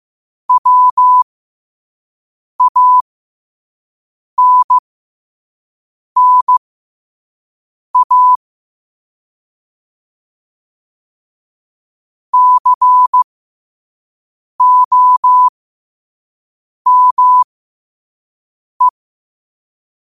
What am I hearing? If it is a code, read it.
Morse code, heard as a keyed tone, about 15 wpm, spaced as for 5 wpm, 1000 Hz. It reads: WANNA COME
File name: 5wpm